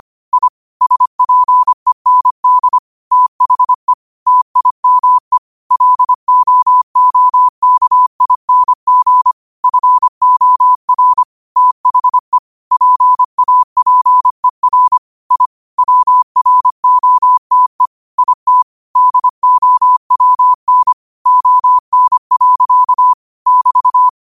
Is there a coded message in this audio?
Yes: Morse code from 0.3 to 24.1 s, heard as a keyed tone, about 25 wpm, 1000 Hz. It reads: I SPEND THE TIME LOOKING FOR THE PAPER I WROTE IT DOWN ON. <BT>